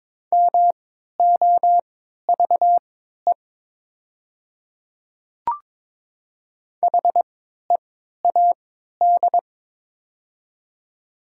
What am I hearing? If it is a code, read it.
Morse code, heard as a keyed tone, about 22 wpm, spaced as for 12 wpm, 700 Hz. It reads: MOVE HEAD
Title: 3x Character Spacing - Top 300 Words - 22wpm